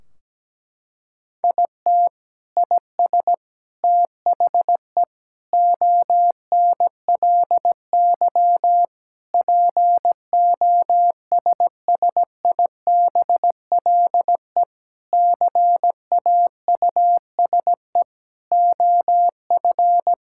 Morse code, heard as a keyed tone, about 17 wpm, 700 Hz. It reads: IT IS THE ONLY POSSIBLE CAUSE OF
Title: Sentences from Top 1600 Words 17wpm